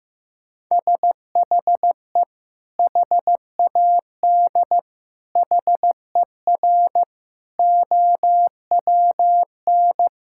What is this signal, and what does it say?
Morse code, heard as a keyed tone, about 15 wpm, 700 Hz. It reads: SHE HAD HER OWN
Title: Sentences from Top 1000 Words - Review Part 5 - 15wpm